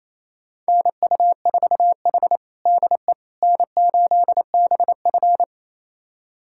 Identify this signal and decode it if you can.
Morse code, heard as a keyed tone, about 28 wpm, 700 Hz. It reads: NU4H DE N8BF
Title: USA Callsign DE Callsign - 4 Characters - 28wpm